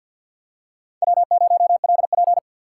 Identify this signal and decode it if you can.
Morse code, heard as a keyed tone, about 52 wpm, 700 Hz. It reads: W0LP
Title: USA Call Signs - 4 Characters - 3x Character Spacing - 50wpm